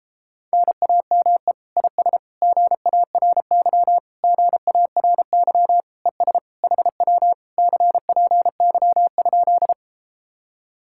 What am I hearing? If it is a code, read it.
Morse code, heard as a keyed tone, about 37 wpm, 700 Hz. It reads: NAME IS GARY GARY ES HW CPY?